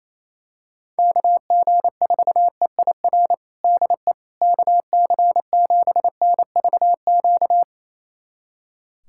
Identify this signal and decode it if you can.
Morse code, heard as a keyed tone, about 28 wpm, 700 Hz. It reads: KG4EIR DE KC7NVQ
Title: USA Callsign DE Callsign 28wpm